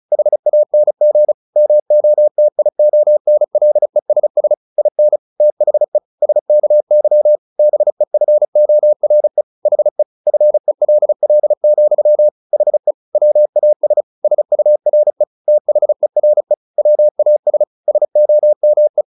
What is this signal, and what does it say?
Morse code, heard as a keyed tone, about 34 wpm, 600 Hz. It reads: HANG MOTIONLESS IN THE SKY BEFORE HE FELL, HE WAS SURE THERE WAS SOME
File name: the old man and the sea 2 35wpm